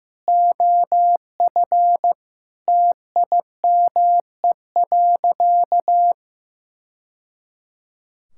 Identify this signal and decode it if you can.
Morse code, heard as a keyed tone, about 15 wpm, 700 Hz. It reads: OF TIME.
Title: Sentences from Top 1000 Words - Review Part 5 - 15wpm